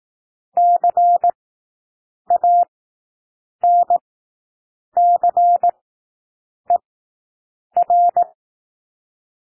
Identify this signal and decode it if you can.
Morse code, heard as a keyed tone, about 18 wpm, spaced as for 7 wpm, 700 Hz. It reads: CANCER
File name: SOUND WORDS 18 wpm